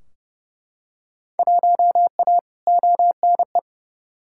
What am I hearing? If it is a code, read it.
Morse code, heard as a keyed tone, about 30 wpm, 700 Hz. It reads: 1A ONE
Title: Field Day Exchanges Encore - 30wpm